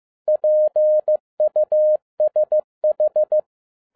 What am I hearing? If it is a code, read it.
Morse code, heard as a keyed tone, about 15 wpm, 610 Hz. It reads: PUSH